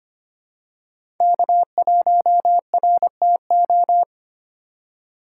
Morse code, heard as a keyed tone, about 25 wpm, 700 Hz. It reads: K1RTO
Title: USA Call Signs - 2x Character Spacing - 25wpm